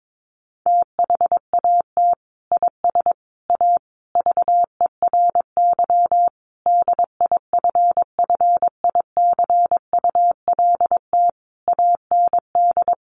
Morse code, heard as a keyed tone, about 22 wpm, 700 Hz. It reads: THAT IS A VERY DIFFICULT AND
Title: Sentences from Top 1200 Words 22wpm